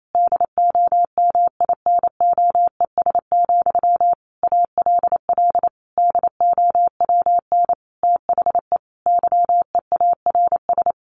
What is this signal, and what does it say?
Morse code, heard as a keyed tone, about 28 wpm, 700 Hz. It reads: DOMINOES, ALL DOWN THE YEARS
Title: A Sound Of Thunder 28wpm